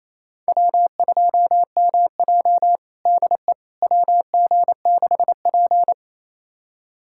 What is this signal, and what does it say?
Morse code, heard as a keyed tone, about 28 wpm, 700 Hz. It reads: W2MJ DE WG6P
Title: USA Callsign DE Callsign - 4 Characters - 28wpm